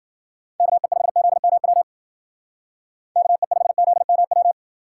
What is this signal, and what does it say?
Morse code, heard as a keyed tone, about 48 wpm, 700 Hz. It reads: KE5ZMW KE5ZMW
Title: Speed Racing - USA 2x3 Call Signs 40wpm